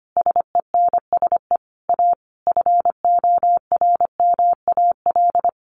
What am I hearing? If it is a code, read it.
Morse code, heard as a keyed tone, about 25 wpm, 700 Hz. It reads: SENSE A FORMAL
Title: Sentences from Top 1300 Words 25wpm